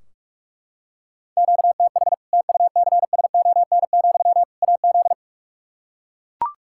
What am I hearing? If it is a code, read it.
Morse code, heard as a keyed tone, about 42 wpm, 700 Hz. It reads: QTH TUCSON, AZ
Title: Practice QSO - Exchange 1 - Beginning - 45wpm